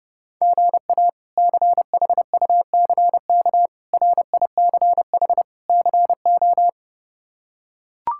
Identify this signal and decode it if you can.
Morse code, heard as a keyed tone, about 30 wpm, 700 Hz. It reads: GA CHUCK RICH CO
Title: K1USN SST QSO Practice - 30wpm